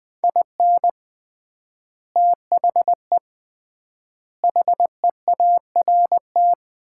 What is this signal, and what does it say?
Morse code, heard as a keyed tone, about 20 wpm, 700 Hz. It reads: IN THE HEART
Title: 3x Word-Spacing - Sets of 5 Words - 20wpm